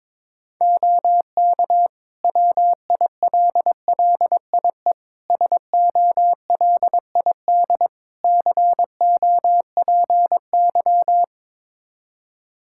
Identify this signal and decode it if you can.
Morse code, heard as a keyed tone, about 22 wpm, 700 Hz. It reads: OK WILLIE SOLID COPY
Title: Practice QSO - Exchange 3 - More Info - 22wpm